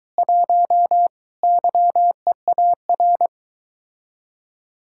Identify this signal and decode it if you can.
Morse code, heard as a keyed tone, about 22 wpm, 700 Hz. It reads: 1 YEAR